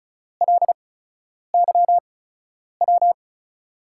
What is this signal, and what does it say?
Morse code, heard as a keyed tone, about 36 wpm, spaced as for 10 wpm, 700 Hz. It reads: LYW
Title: USA Call Signs - 5 Characters - 8x Character Spacing - 35wpm